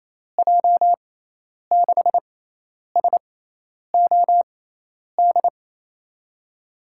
Morse code, heard as a keyed tone, about 28 wpm, 700 Hz. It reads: J 6 S O D
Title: USA Call Signs - 6x Character Spacing - 28wpm